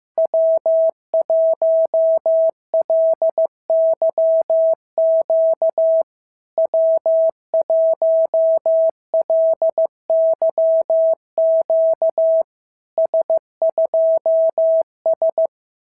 Morse code, heard as a keyed tone, about 14 wpm, 650 Hz. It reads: W1LYQ W1LYQ S2S
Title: SOTA QSO Practice 15wpm